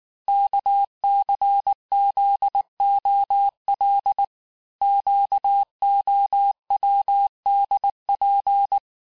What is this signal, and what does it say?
Morse code, heard as a keyed tone, about 19 wpm, 800 Hz. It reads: KCZOL QOWDP
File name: cours-du-mercredi / audios / cours-869-texte-CHIFFRE-vitesse-18-MPM